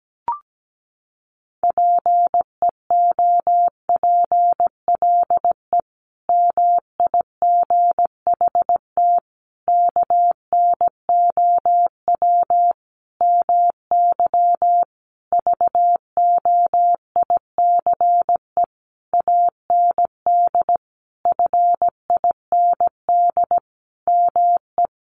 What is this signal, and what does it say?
Morse code, heard as a keyed tone, about 17 wpm, 700 Hz. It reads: PEOPLE MIGHT KNOW MY VOICE AND FIND ME
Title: Sentences from Top 500 Words 17wpm